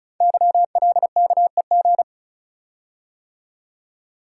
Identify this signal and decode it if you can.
Morse code, heard as a keyed tone, about 36 wpm, 700 Hz. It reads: YLKEG
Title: Code Groups - 5 Letters - 35wpm